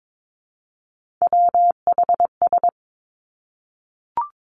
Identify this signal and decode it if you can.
Morse code, heard as a keyed tone, about 22 wpm, 700 Hz. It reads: WHS
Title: Three Challenging Letter-Number Combinations 22wpm